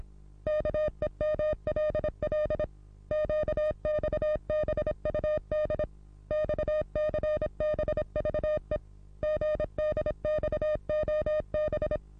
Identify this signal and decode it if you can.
Morse code, heard as a keyed tone, about 26 wpm, 610 Hz. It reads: KEMLL QXBUD XCBVE GDXOB
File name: repos / Morsekurs / DJ4UF Morsen27